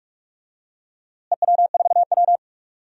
Morse code, heard as a keyed tone, about 46 wpm, 690 Hz. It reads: EW4W
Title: DXCC-100 Call Signs - 4 Characters - Repeated 3x - 45wpm